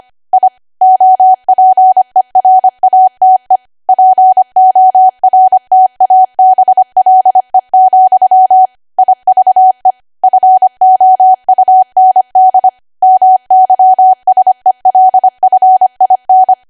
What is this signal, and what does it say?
Morse code, heard as a keyed tone, about 25 wpm, 750 Hz. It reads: I OPERATE PORTABLE, IVE FOUND MYSELF IN
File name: Archive / 25 WPM / 25WPM